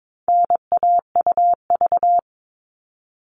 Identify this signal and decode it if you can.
Morse code, heard as a keyed tone, about 22 wpm, 700 Hz. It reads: NAUV
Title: All Cut Numbers - 4-Digit 22wpm